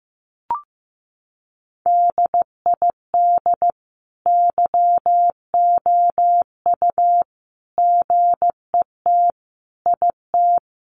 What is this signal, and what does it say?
Morse code, heard as a keyed tone, about 15 wpm, 700 Hz. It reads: DID YOU GET IT
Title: Sentences from Top 1300 Words 15wpm